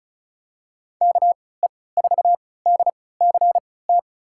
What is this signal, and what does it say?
Morse code, heard as a keyed tone, about 34 wpm, spaced as for 20 wpm, 700 Hz. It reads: KE4DCT
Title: USA Call Signs - 6 Characters - 3x Character Spacing - 35wpm